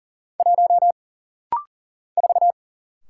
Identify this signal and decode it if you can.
Morse code, heard as a keyed tone, about 40 wpm, 700 Hz. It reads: 1 4
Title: Single Letter-Number Alt - Rapid-Fire 40wpm